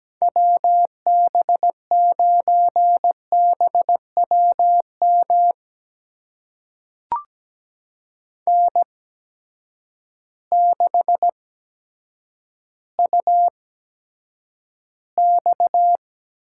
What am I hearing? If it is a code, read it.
Morse code, heard as a keyed tone, about 17 wpm, spaced as for 9 wpm, 700 Hz. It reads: WB9BWM N6UX
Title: USA Call Signs - 8x Character Spacing - 17wpm